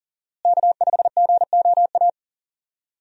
Morse code, heard as a keyed tone, about 40 wpm, 700 Hz. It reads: K5GOA
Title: USA Call Signs - 5 Characters - Repeated 3x - 40wpm